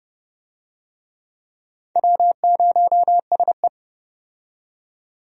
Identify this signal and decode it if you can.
Morse code, heard as a keyed tone, about 31 wpm, 700 Hz. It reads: W0SE
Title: USA Call Signs - 4 Characters - 30wpm